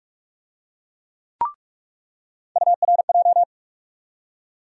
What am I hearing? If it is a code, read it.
Morse code, heard as a keyed tone, about 46 wpm, 700 Hz. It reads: URJ
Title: USA Call Sign Postfixes - 3 Letters 45wpm